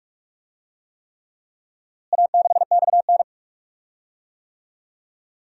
Morse code, heard as a keyed tone, about 46 wpm, 700 Hz. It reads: A6XN
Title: DXCC-100 Call Signs - 4 Characters - Repeated 3x - 45wpm